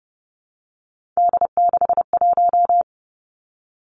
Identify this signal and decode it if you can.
Morse code, heard as a keyed tone, about 32 wpm, 700 Hz. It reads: D61
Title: Three Challenging Letter-Number Combinations 30wpm